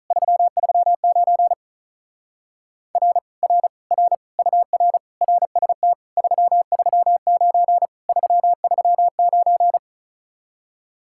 Morse code, heard as a keyed tone, about 44 wpm, 700 Hz. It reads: 339 R R R UR RST 339 339